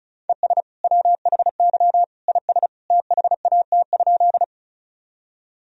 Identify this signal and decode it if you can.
Morse code, heard as a keyed tone, about 36 wpm, 700 Hz. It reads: ES WHY IS THAT?
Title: Sentences from Top 200 Words and Abbreviations 35wpm